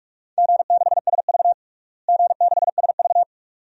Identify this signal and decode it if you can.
Morse code, heard as a keyed tone, about 41 wpm, 700 Hz. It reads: G6SV G6SV
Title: DXCC-100 Call Signs - 4 Characters - Repeated 2x - 45wpm